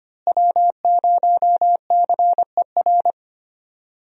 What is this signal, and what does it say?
Morse code, heard as a keyed tone, about 25 wpm, 700 Hz. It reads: W0CER
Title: USA Call Signs - 5 Characters - Repeated 3x - 25wpm